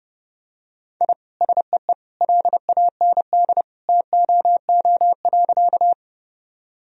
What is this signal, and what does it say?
Morse code, heard as a keyed tone, about 30 wpm, 700 Hz. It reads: I SEE LAND TOO.
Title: Sentences from Top 1000 Words - Review Part 1 - 30wpm